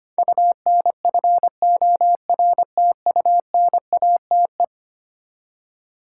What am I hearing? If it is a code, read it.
Morse code, heard as a keyed tone, about 25 wpm, 700 Hz. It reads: UNFORTUNATE
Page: morse-code-ninja-practice-25wpm